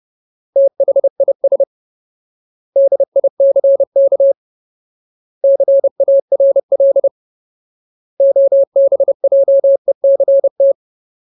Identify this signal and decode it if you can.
Morse code, heard as a keyed tone, about 30 wpm, 550 Hz. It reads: THIS DICK CARL OBJECT
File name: Ultimate Word List / 30WPM Ultimate